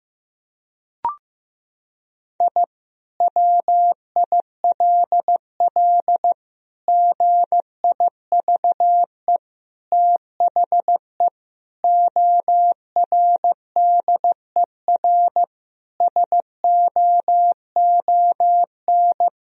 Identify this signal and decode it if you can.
Morse code, heard as a keyed tone, about 16 wpm, 700 Hz. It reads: I WILL GIVE THE ORDER SOON
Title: Sentences from Top 1000 Words - Review Part 2 - 15wpm